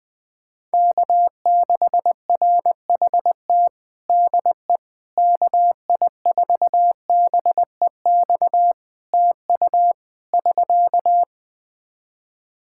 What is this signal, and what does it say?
Morse code, heard as a keyed tone, about 20 wpm, 700 Hz. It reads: K6RHT DE KI4BEX TU <SK>
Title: K1USN SST QSO Practice 20wpm